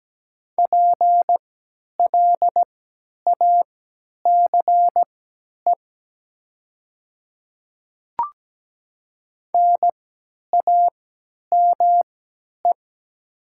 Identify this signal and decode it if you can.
Morse code, heard as a keyed tone, about 17 wpm, spaced as for 10 wpm, 700 Hz. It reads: PLACE NAME
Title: 3x Character Spacing - Top 300 Words - 17wpm